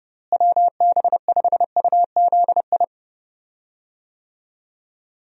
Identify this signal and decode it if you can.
Morse code, heard as a keyed tone, about 30 wpm, 700 Hz. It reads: WB5UZI
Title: USA Call Signs - 6 Characters - Repeated 3x - 30wpm